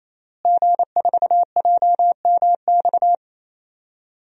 Morse code, heard as a keyed tone, about 28 wpm, 700 Hz. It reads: G4JMX
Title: DXCC-100 Call Signs - 5 Characters - Repeated 3x - 28wpm